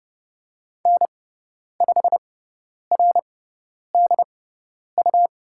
Morse code, heard as a keyed tone, about 30 wpm, spaced as for 10 wpm, 700 Hz. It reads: N5RDU
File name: Lesson 18g - Letter m - Call Signs 30wpm